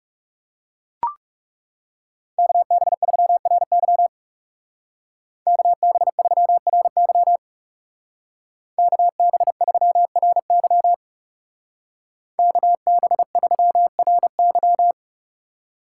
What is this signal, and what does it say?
Morse code, heard as a keyed tone, about 38 wpm, 700 Hz. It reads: KB3RY KB3RY KB3RY KB3RY